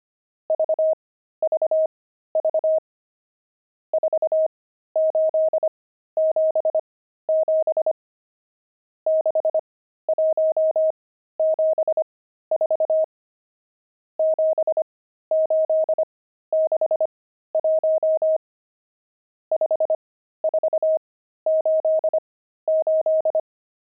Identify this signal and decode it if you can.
Morse code, heard as a keyed tone, about 25 wpm, spaced as for 13 wpm, 630 Hz. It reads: VVV 4877 6174 7861 5488
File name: pangram-fs-13-cs-25-nums15